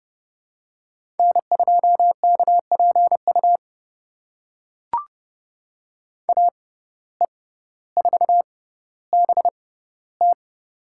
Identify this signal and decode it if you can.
Morse code, heard as a keyed tone, about 30 wpm, spaced as for 15 wpm, 700 Hz. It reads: N2KPU AE4BT
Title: USA Call Signs - 5 Characters - 6x Character Spacing - 30wpm